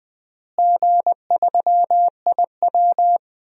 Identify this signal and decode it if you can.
Morse code, heard as a keyed tone, about 20 wpm, 700 Hz. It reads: G3IW
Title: DXCC-100 Call Signs - 4 Characters - Repeated 3x - 20wpm